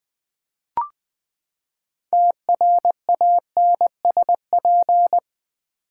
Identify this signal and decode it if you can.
Morse code, heard as a keyed tone, about 20 wpm, 700 Hz. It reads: TRANSP
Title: Build A Word - 6 Letters At A Time - 12-Letter Words - 20wpm